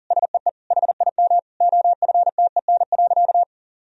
Morse code, heard as a keyed tone, about 37 wpm, 700 Hz. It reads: SEE HIM OFTEN.
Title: Sentences from Top 1000 Words - Review Part 1 - 40wpm